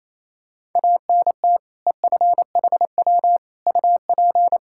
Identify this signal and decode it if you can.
Morse code, heard as a keyed tone, about 27 wpm, 700 Hz. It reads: ANT EFHW UP
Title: Practice QSO - Exchange 2 - Info - 28wpm